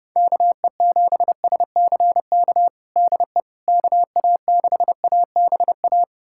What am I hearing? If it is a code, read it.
Morse code, heard as a keyed tone, about 30 wpm, 700 Hz. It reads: KE7SCK DE KA6ABA
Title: USA Callsign DE Callsign - 6 Characters - 30wpm